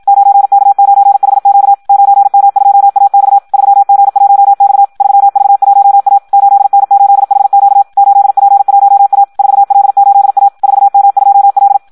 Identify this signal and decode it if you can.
Morse code, heard as a keyed tone, about 56 wpm, 800 Hz. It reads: 9K9HQ 9M2AX 3G1X 3V1A 8N8HQ 8P1A 4L8A 4M2L
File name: Audio practica CW / qrq-tx-54 2